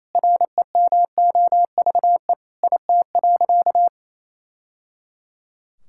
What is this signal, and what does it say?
Morse code, heard as a keyed tone, about 27 wpm, 700 Hz. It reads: REMOVE IT.
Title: Sentences from Top 1000 Words - Review Part 5 - 28wpm